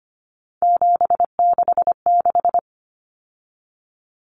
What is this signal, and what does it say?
Morse code, heard as a keyed tone, about 25 wpm, 700 Hz. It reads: 766
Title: Common Cut Numbers - 3-Digit 25wpm